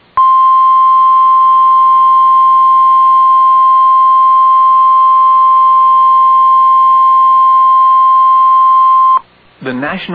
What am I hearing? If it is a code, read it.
Emergency Alert System (SAME) tones (the 1050 Hz attention tone), with speech. It is heard from 0.4 to 9.0 s.